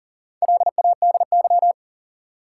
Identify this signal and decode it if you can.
Morse code, heard as a keyed tone, about 37 wpm, 700 Hz. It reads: LADY
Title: Four Letter Words - 40wpm